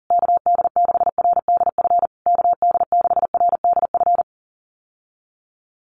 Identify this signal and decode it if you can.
Morse code, heard as a keyed tone, about 40 wpm, 700 Hz. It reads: KD6RDF KD6RDF
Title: USA Call Signs Repeated 3x 40wpm